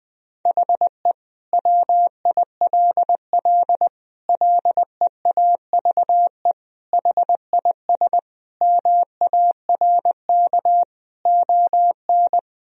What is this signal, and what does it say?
Morse code, heard as a keyed tone, about 21 wpm, 700 Hz. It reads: HE WILL LEAVE HIS MARK ON
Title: Sentences from Top 1000 Words - Review Part 1 - 20wpm